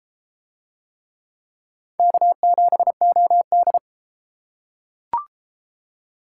Morse code, heard as a keyed tone, about 33 wpm, 700 Hz. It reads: K7OD